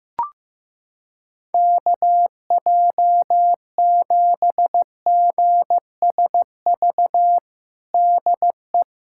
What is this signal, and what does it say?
Morse code, heard as a keyed tone, about 15 wpm, 700 Hz. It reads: KJ7GSV DE
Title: USA Callsign DE Callsign - 6 Characters - 15wpm